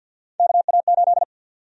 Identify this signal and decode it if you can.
Morse code, heard as a keyed tone, about 45 wpm, 700 Hz. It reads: KA8
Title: USA Call Sign Prefixes - 50wpm